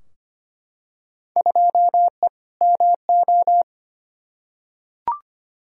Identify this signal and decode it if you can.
Morse code, heard as a keyed tone, about 25 wpm, 700 Hz. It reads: 2E MO
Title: Field Day Exchanges - 25wpm